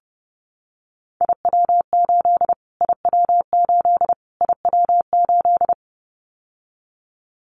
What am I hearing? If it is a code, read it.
Morse code, heard as a keyed tone, about 28 wpm, 700 Hz. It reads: IW8 IW8 IW8
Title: Callsign Prefixes for Top 25 DXCC - Repeated 3x - 30wpm